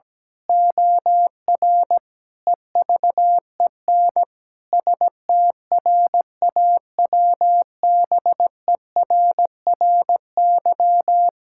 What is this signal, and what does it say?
Morse code, heard as a keyed tone, about 17 wpm, 700 Hz. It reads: OR EVEN STRAWBERRY
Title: Pangram Challenge v2 - 17wpm